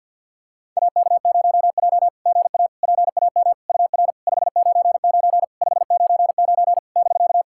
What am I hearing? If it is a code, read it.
Morse code, heard as a keyed tone, about 49 wpm, 690 Hz. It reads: AK0J GA PAM UR 599 599 <BK>
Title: SOTA QSO Practice 50wpm